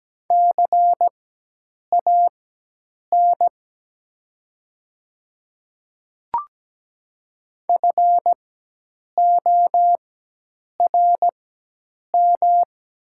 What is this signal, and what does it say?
Morse code, heard as a keyed tone, about 17 wpm, spaced as for 8 wpm, 690 Hz. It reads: CAN FORM
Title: 4x Character Spacing - Top 300 Words - 17wpm